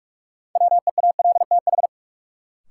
Morse code, heard as a keyed tone, about 43 wpm, 700 Hz. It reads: WEALTH
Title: Sentences from Top 1200 Words - New Word Drill Alt - 45wpm